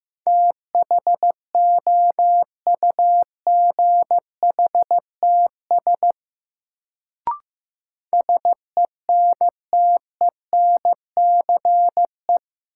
Morse code, heard as a keyed tone, about 15 wpm, 700 Hz. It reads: THOUGHTS SENTENCE
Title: Sentences from Top 1000 Words - Review Part 1 - New Word Drill - 15wpm